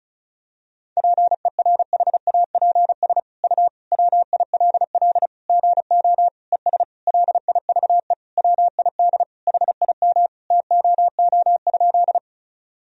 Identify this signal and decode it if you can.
Morse code, heard as a keyed tone, about 34 wpm, 700 Hz. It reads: PERHAPS U WILL GO ES LIVE WID HIM TOO?
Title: Sentences from Top 500 Words and Abbreviations 35wpm